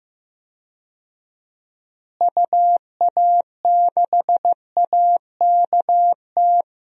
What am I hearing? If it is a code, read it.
Morse code, heard as a keyed tone, about 15 wpm, 700 Hz. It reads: UA6AKT
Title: DXCC-100 Call Signs - 6 Characters - Repeated 3x - 15wpm